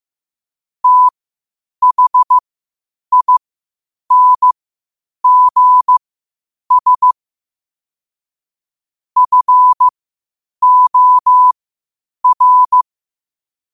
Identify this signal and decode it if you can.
Morse code, heard as a keyed tone, about 15 wpm, spaced as for 8 wpm, 1000 Hz. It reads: THINGS FOR
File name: qotdfiles / 8wpm